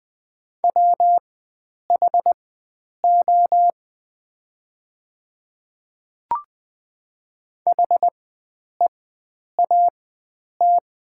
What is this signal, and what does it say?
Morse code, heard as a keyed tone, about 20 wpm, spaced as for 9 wpm, 700 Hz. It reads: WHO HEAT
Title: 4x Character Spacing - Top 500 Words - 20wpm